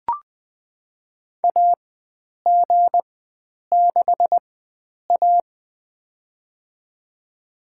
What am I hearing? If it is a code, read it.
Morse code, heard as a keyed tone, about 20 wpm, spaced as for 9 wpm, 700 Hz. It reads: AG6A
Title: USA Call Signs - 4 Characters - 4x Character Spacing - 20wpm